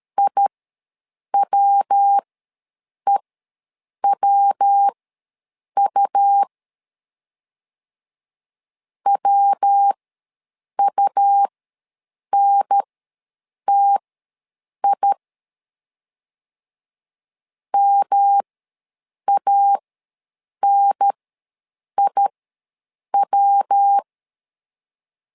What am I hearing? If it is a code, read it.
Morse code, heard as a keyed tone, about 13 wpm, spaced as for 7 wpm, 800 Hz. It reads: IWEWU WUNTI MANIW
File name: LEZIONI / 3 MWRO / 3 MW ws 12wpm 1816